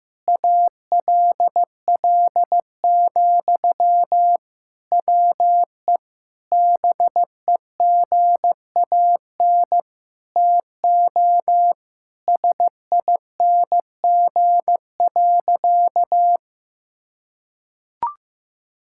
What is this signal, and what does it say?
Morse code, heard as a keyed tone, about 15 wpm, 700 Hz. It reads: ALL, WE BEGAN TO SING.
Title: Sentences from Top 1600 Words 15wpm